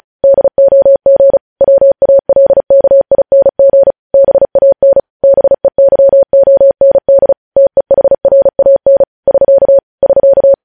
Morse code, heard as a keyed tone, about 36 wpm, 550 Hz. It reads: DOG WALKING BAN BEYOND TEHRAN <SK> <SK>
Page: news headlines in morse code at 35 wpm